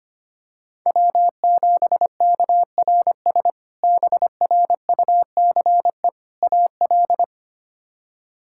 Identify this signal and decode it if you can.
Morse code, heard as a keyed tone, about 25 wpm, 700 Hz. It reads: W7KRS BRUCE AL
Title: K1USN SST QSO Practice - 25wpm